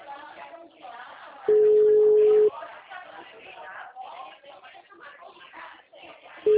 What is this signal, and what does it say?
Ringback tone (Europe, 425 Hz), with speech.